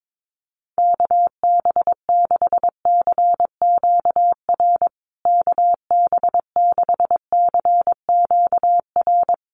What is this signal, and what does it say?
Morse code, heard as a keyed tone, about 22 wpm, 700 Hz. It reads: KB6CQR KB6CQR
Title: USA Call Signs Repeated 3x 22wpm